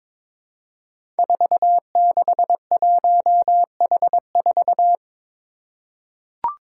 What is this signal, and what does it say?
Morse code, heard as a keyed tone, about 22 wpm, 700 Hz. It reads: 461H4
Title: Five Challenging Letter-Number Combinations 22wpm